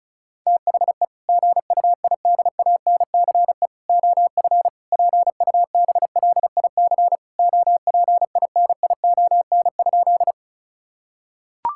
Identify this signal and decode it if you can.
Morse code, heard as a keyed tone, about 36 wpm, 700 Hz. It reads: THE GUIDANCE OF PUBLIC OPINION?
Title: Sentences from Top 1600 Words 35wpm